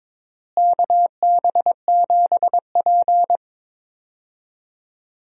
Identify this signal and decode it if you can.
Morse code, heard as a keyed tone, about 22 wpm, 700 Hz. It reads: KB7P
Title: USA Call Signs - 4 Characters - Repeated 3x - 22wpm